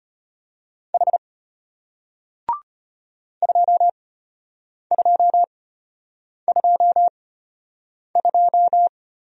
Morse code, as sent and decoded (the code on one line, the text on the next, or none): .... / ..--- ..--- ..--- ..---
H 2222